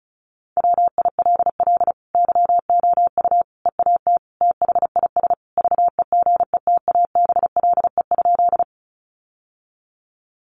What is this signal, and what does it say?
Morse code, heard as a keyed tone, about 34 wpm, 700 Hz. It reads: WILL YOU EAT THIS VEGETABLE?
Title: Sentences from Top 1300 Words 35wpm